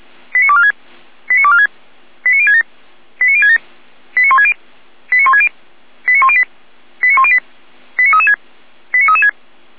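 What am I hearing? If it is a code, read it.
Five-tone selective call (ZVEI), heard as quick runs of five tones, tones of about 70 ms. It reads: 89246 89246 89067 89067 89160 89160 89108 89108 89336 89336
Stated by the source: ZVEI five-tone selective call received on SDR.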